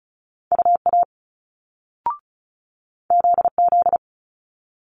Morse code, heard as a keyed tone, about 36 wpm, 700 Hz. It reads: UA ZZ
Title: Multi-letter Phonemes Quick Fire 35wpm